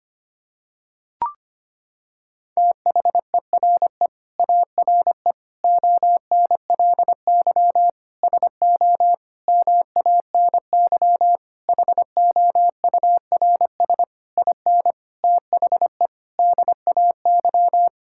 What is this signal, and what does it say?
Morse code, heard as a keyed tone, about 25 wpm, 700 Hz. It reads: THERE ARE ONLY SO MANY HOURS IN THE DAY
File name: Sets of 9 Words - 25wpm